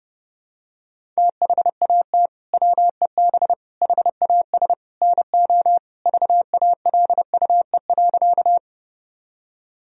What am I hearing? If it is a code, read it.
Morse code, heard as a keyed tone, about 32 wpm, 700 Hz. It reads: THAT WEB HAS NO VALUE.
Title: Sentences from Top 1000 Words - Review Part 4 - 30wpm